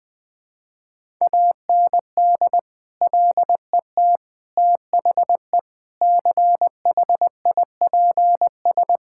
Morse code, heard as a keyed tone, about 20 wpm, 700 Hz. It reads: AND LET THE CHIPS
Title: Sets of 8 Words - 20wpm